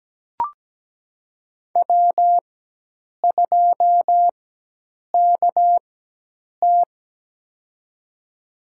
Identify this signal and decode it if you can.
Morse code, heard as a keyed tone, about 17 wpm, spaced as for 8 wpm, 700 Hz. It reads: W2KT
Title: USA Call Signs - 4 Characters - 4x Character Spacing - 17wpm